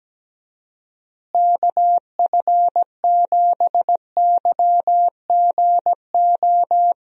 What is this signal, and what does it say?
Morse code, heard as a keyed tone, about 17 wpm, 700 Hz. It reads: KF7YGO
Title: USA Call Signs - 6 Characters - Repeated 2x - 17wpm